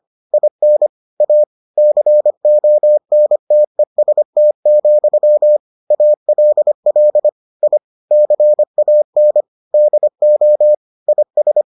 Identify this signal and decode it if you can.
Morse code, heard as a keyed tone, about 25 wpm, 600 Hz. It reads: IN A CONTEST, ALL I CAN DO IS